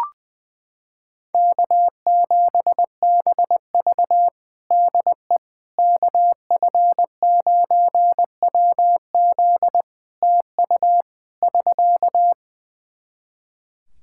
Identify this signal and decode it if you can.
Morse code, heard as a keyed tone, about 20 wpm, 700 Hz. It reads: K7BV DE KF9WZ TU <SK>
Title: K1USN SST QSO Practice 20wpm